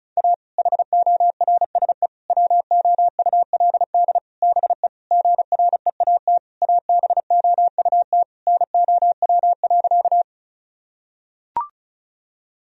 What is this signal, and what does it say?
Morse code, heard as a keyed tone, about 34 wpm, 700 Hz. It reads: A HORSE WOULD BE GREAT ABOUT NOW.
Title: Sentences from Top 1000 Words - Review Part 1 - 35wpm